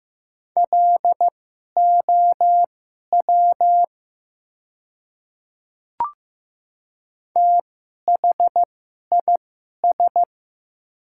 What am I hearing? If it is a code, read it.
Morse code, heard as a keyed tone, about 15 wpm, spaced as for 11 wpm, 690 Hz. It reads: LOW THIS
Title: 2x Character Spacing - Top 300 Words - 15wpm